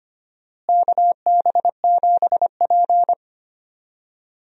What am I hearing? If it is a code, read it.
Morse code, heard as a keyed tone, about 25 wpm, 700 Hz. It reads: KB7P
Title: USA Call Signs - 4 Characters - Repeated 3x - 25wpm